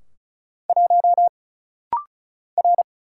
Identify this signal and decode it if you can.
Morse code, heard as a keyed tone, about 36 wpm, 700 Hz. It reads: 1 R
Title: Single Letter-Number Alt - Rapid-Fire 35wpm